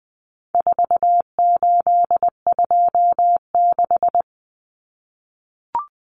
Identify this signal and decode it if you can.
Morse code, heard as a keyed tone, about 20 wpm, 700 Hz. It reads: 4826
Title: Numbers - 4-Digit 20wpm